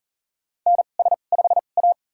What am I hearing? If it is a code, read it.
Morse code, heard as a keyed tone, about 40 wpm, spaced as for 29 wpm, 700 Hz. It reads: NS5A
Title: USA Call Signs - 4 Characters - 2x Character Spacing - 40wpm